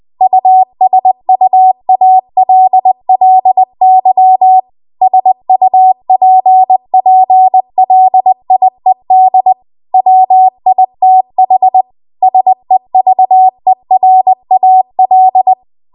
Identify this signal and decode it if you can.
Morse code, heard as a keyed tone, about 20 wpm, 750 Hz. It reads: USUALLY SUPPLIED WITH SEVERAL
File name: Archive / 20 WPM / 20WPM